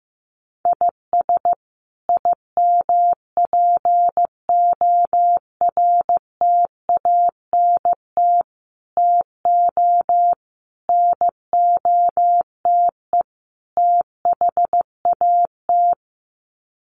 Morse code, heard as a keyed tone, about 15 wpm, 700 Hz. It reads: IS IMPORTANT TO NOTE THAT
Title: Sets of 5 Words 15wpm